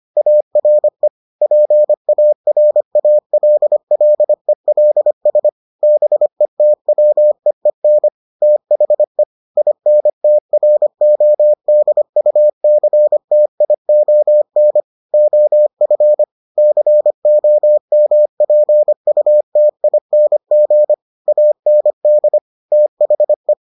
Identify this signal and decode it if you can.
Morse code, heard as a keyed tone, about 25 wpm, 600 Hz. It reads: ARE PARALLELS BETWEEN THE INTRODUCTION OF COMPUTING AND THE